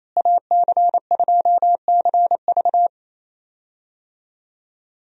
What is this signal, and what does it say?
Morse code, heard as a keyed tone, about 28 wpm, 700 Hz. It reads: AC2CV
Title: USA Call Signs - 5 Characters - Repeated 3x - 28wpm